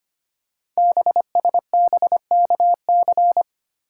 Morse code, heard as a keyed tone, about 25 wpm, 700 Hz. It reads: BSBKC
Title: Code Groups - 5 Letters - 25wpm